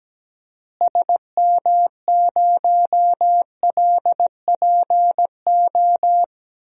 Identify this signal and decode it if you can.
Morse code, heard as a keyed tone, about 17 wpm, 700 Hz. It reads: SM0LPO
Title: DXCC-100 Call Signs - 6 Characters - Repeated 2x - 17wpm